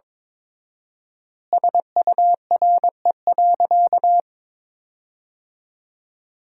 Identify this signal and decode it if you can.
Morse code, heard as a keyed tone, about 22 wpm, 700 Hz. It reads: SURE.
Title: 4x Word-Spacing - Sentences from Top 500 Words - 22wpm